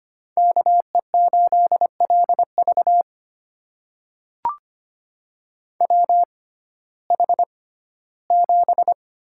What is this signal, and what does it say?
Morse code, heard as a keyed tone, about 25 wpm, spaced as for 15 wpm, 700 Hz. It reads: KE8LV WH7
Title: USA Call Signs - 5 Characters - 6x Character Spacing - 25wpm